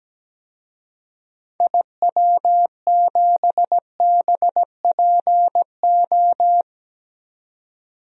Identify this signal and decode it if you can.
Morse code, heard as a keyed tone, about 17 wpm, 700 Hz. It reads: IW7BPO
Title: DXCC-100 Call Signs - 6 Characters - 17wpm